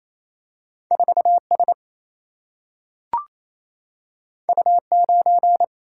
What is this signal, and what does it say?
Morse code, heard as a keyed tone, about 28 wpm, 700 Hz. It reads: 4 S U 9
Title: Single-Letter Single-Number 28wpm